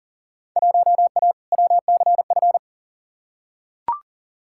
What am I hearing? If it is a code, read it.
Morse code, heard as a keyed tone, about 40 wpm, 700 Hz. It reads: 1A WCF
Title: Field Day Exchanges - 40wpm